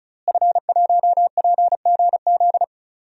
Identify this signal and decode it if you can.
Morse code, heard as a keyed tone, about 34 wpm, 700 Hz. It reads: F1PGZ
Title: DXCC-100 Call Signs - 5 Characters - Repeated 2x - 35wpm